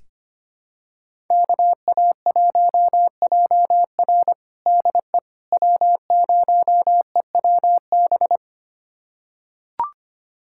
Morse code, heard as a keyed tone, about 25 wpm, 700 Hz. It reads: KA1JR DE W0EWB
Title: USA Callsign DE Callsign - 5 Characters - 25wpm